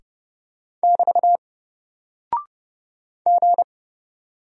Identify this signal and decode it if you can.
Morse code, heard as a keyed tone, about 31 wpm, 700 Hz. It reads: <BT> G
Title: Single Character (Letter:Number:Punctuation:Prosign) 30wpm